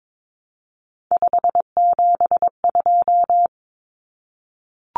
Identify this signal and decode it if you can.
Morse code, heard as a keyed tone, about 22 wpm, 700 Hz. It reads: 572
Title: Common Cut Numbers - 3-Digit 22wpm